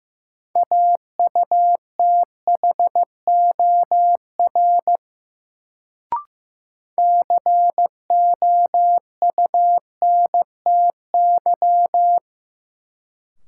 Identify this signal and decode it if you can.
Morse code, heard as a keyed tone, about 15 wpm, 700 Hz. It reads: AUTHOR COUNTY
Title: Sentences from Top 1200 Words - New Word Drill Alt - 15wpm